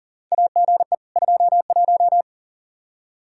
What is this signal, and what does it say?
Morse code, heard as a keyed tone, about 37 wpm, 700 Hz. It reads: AGE 21
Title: Practice QSO - Exchange 3 - More Info - 40wpm